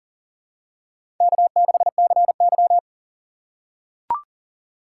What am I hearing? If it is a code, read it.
Morse code, heard as a keyed tone, about 40 wpm, 700 Hz. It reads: K6CY
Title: USA Call Signs - 4 Characters - 2x Character Spacing - 40wpm